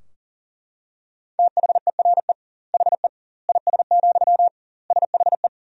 Morse code, heard as a keyed tone, about 40 wpm, 700 Hz. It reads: THERE HE IS, SHE
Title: 2x Word-Spacing - Sentences from Top 100 Words - 40wpm